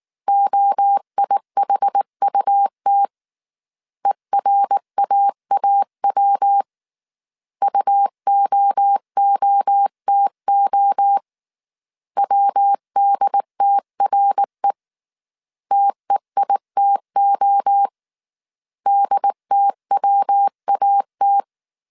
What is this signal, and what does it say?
Morse code, heard as a keyed tone, about 19 wpm, 800 Hz. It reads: OIHUT ERAAW UOOTO WDTRE TEITO DTWAT
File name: CW / DET / 1 3 1500C S B 18wpm 1916